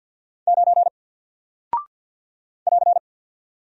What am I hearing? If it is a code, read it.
Morse code, heard as a keyed tone, about 48 wpm, 700 Hz. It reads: 9 <AR>
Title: Single Character (Letter:Number:Punctuation:Prosign) 50wpm